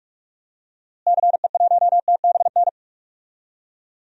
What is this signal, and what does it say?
Morse code, heard as a keyed tone, about 44 wpm, 700 Hz. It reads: CE1TBN
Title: DXCC-100 Call Signs - 6 Characters - Repeated 3x - 45wpm